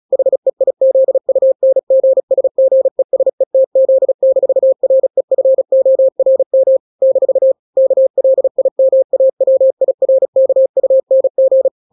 Morse code, heard as a keyed tone, about 36 wpm, 530 Hz. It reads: HEIZUNGSGESETZ-REFORM - KLIMAWIRKUNG
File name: daily / 35wpm